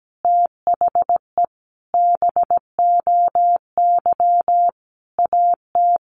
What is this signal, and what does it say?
Morse code, heard as a keyed tone, about 17 wpm, 700 Hz. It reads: THE BOY AT
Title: Sentences from Top 300 Words 17wpm